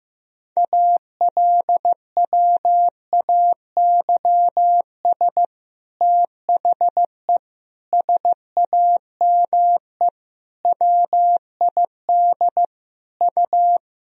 Morse code, heard as a keyed tone, about 15 wpm, 700 Hz. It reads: ALWAYS THE SAME WID U
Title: Sentences from Top 300 Words and Abbreviations 15wpm